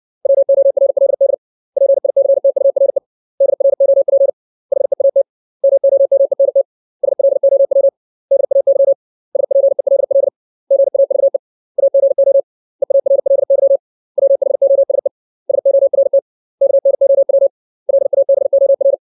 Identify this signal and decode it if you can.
Morse code, heard as a keyed tone, about 60 wpm, 550 Hz. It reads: WORLD PICTURE DROP HEAT MOMENT SLOW DAY SPELL GAVE AGO EARLY WHOSE SORT KNOW LABOR
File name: 632 words / 632 Words 60WPM